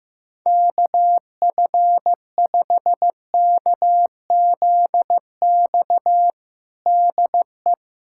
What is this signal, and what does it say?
Morse code, heard as a keyed tone, about 15 wpm, 700 Hz. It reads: KF5KZX DE
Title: USA Callsign DE Callsign 15wpm